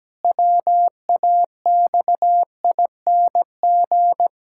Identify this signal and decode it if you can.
Morse code, heard as a keyed tone, about 17 wpm, 700 Hz. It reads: WAXING
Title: Pangram Challenge v2 - 17wpm